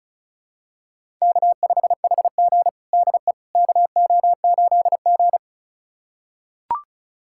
Morse code, heard as a keyed tone, about 36 wpm, 700 Hz. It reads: K5HG DE KO8G
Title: USA Callsign DE Callsign - 4 Characters - 35wpm